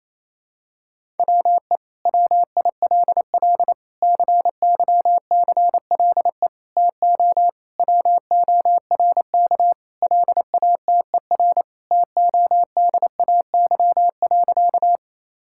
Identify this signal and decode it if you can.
Morse code, heard as a keyed tone, about 28 wpm, 700 Hz. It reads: WE WILL CYCLE TO WORK LATER TODAY.
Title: Sentences from Top 1000 Words - Review Part 6 - 28wpm